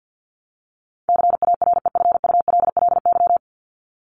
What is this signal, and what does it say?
Morse code, heard as a keyed tone, about 50 wpm, 700 Hz. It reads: CAREFULLY
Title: LY Suffix - Top 100 Words with Base Words 50wpm